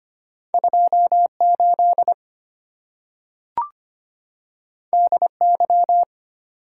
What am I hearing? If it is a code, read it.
Morse code, heard as a keyed tone, about 25 wpm, 700 Hz. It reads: 28 DY
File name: Two Letter-Number Combinations - 25wpm